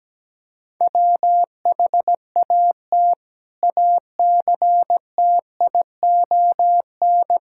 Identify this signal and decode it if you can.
Morse code, heard as a keyed tone, about 17 wpm, 700 Hz. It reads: WHAT ACTION
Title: Sentences from Top 1000 Words - Review Part 6 - 17wpm